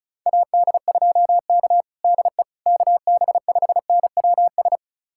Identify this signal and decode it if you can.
Morse code, heard as a keyed tone, about 34 wpm, 700 Hz. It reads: AD2K DE KB5NWS
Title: USA Callsign DE Callsign 35wpm